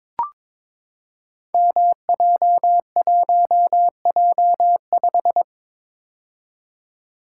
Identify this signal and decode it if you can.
Morse code, heard as a keyed tone, about 22 wpm, 700 Hz. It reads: MJ1J5
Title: Code Groups - 5 Letter:Numbers - 22wpm